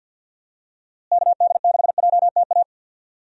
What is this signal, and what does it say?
Morse code, heard as a keyed tone, about 51 wpm, 700 Hz. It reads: KD6JTA
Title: USA Call Signs - 6 Characters - Repeated 3x - 50wpm